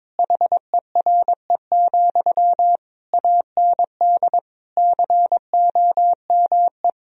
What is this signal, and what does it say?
Morse code, heard as a keyed tone, about 22 wpm, 700 Hz. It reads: HERE, AND COME
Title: Sentences from Top 1000 Words - Review Part 2 - 22wpm